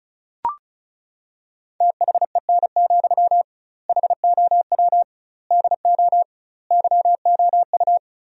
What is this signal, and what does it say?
Morse code, heard as a keyed tone, about 34 wpm, 700 Hz. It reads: THEN, HOW DO YOU
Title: 2x Word-Spacing - Sentences from Top 100 Words - 35wpm